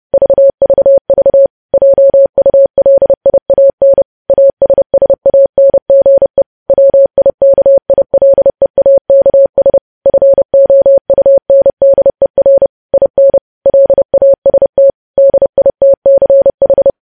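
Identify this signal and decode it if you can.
Morse code, heard as a keyed tone, about 30 wpm, 550 Hz. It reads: VVV JULIAN ASSANGE WIKILEAKS FOUNDER IN LAST DITCH
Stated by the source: News headlines sent in Morse at 30.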